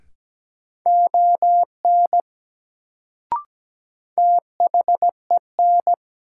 Morse code, heard as a keyed tone, about 17 wpm, 700 Hz. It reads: ON THEN
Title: Sentences from Top 100 Words - New Word Drill - 17wpm